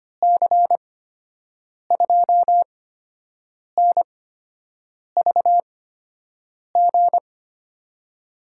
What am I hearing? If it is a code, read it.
Morse code, heard as a keyed tone, about 25 wpm, spaced as for 7 wpm, 700 Hz. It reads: C2NVG
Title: USA Call Signs - 6 Characters - 8x Character Spacing - 25wpm